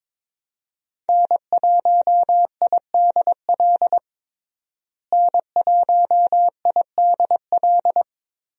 Morse code, heard as a keyed tone, about 21 wpm, 700 Hz. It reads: N1IDL N1IDL
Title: USA Call Signs - 5 Characters - Repeated 2x - 22wpm